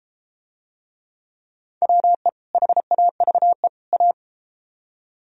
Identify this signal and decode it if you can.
Morse code, heard as a keyed tone, about 34 wpm, 700 Hz. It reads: WE HAVE A